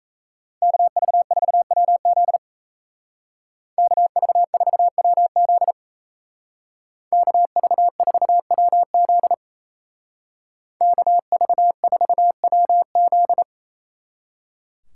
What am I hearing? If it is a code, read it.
Morse code, heard as a keyed tone, about 37 wpm, 700 Hz. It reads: KV4WZ KV4WZ KV4WZ KV4WZ